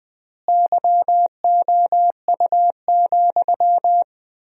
Morse code, heard as a keyed tone, about 20 wpm, 700 Hz. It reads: YOU,
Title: 4x Word-Spacing - Sentences from Top 400 Words - 20wpm